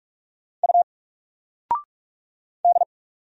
Morse code, heard as a keyed tone, about 44 wpm, 700 Hz. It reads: U D
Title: Single Letter-Number 45wpm